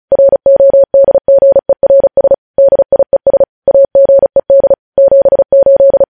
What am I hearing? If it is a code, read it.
Morse code, heard as a keyed tone, about 36 wpm, 550 Hz. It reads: RODGERS DIES AGED 78
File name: podcast / News headlines-1x-35x35